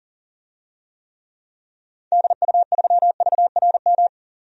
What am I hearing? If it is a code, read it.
Morse code, heard as a keyed tone, about 40 wpm, 700 Hz. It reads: DU3VRM
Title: DXCC-100 Call Signs - 6 Characters - 40wpm